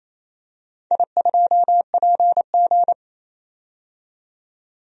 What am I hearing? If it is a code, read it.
Morse code, heard as a keyed tone, about 28 wpm, 700 Hz. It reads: I2PG
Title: DXCC-100 Call Signs - 4 Characters - 28wpm